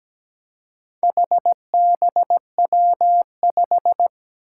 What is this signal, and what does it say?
Morse code, heard as a keyed tone, about 17 wpm, 700 Hz. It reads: HBW5
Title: Four Challenging Letter-Number Combinations 17wpm